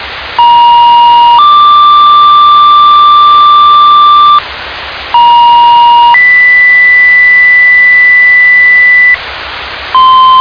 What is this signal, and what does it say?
Two-tone page, heard as a one-second tone, then a three-second tone, about 950 then 1180 Hz, about 950 then 2000 Hz, with noise. It is heard from 0.4 to 9.2 s.